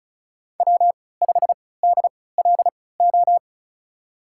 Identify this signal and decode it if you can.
Morse code, heard as a keyed tone, about 36 wpm, spaced as for 20 wpm, 700 Hz. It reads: W5DLO
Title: USA Call Signs - 3x Character Spacing - 35wpm